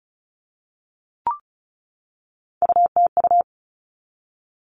Morse code, heard as a keyed tone, about 34 wpm, 700 Hz. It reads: UTU
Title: All Cut Numbers - 3-Digit 35wpm